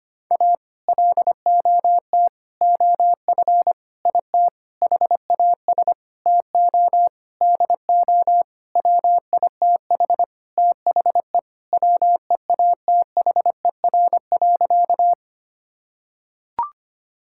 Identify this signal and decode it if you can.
Morse code, heard as a keyed tone, about 25 wpm, 700 Hz. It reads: A LOT OF IT HAS TO DO WITH THE WEATHER.
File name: practice / 25wpm / Sentences with 9-Word Phrases - 25wpm